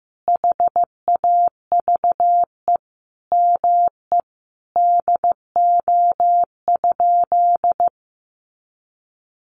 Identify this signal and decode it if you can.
Morse code, heard as a keyed tone, about 15 wpm, 700 Hz. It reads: HAVE ME DO?
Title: Sentences from Top 200 Words 15wpm